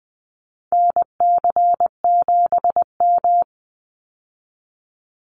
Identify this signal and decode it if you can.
Morse code, heard as a keyed tone, about 20 wpm, 700 Hz. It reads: NC7M
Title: Call Signs Repeated 3x - 20wpm